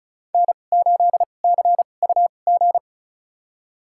Morse code, heard as a keyed tone, about 36 wpm, spaced as for 25 wpm, 700 Hz. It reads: N8CUG
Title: USA Call Signs - 5 Characters - 2x Character Spacing - 35wpm